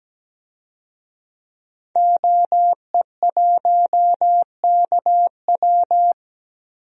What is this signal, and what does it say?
Morse code, heard as a keyed tone, about 17 wpm, 700 Hz. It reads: OE1KW
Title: DXCC-100 Call Signs - 5 Characters - 17wpm